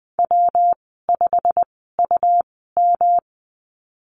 Morse code, heard as a keyed tone, about 20 wpm, spaced as for 15 wpm, 700 Hz. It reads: W5UM
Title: 128 Lesson 21g - Number 6 - Call Signs 20wpm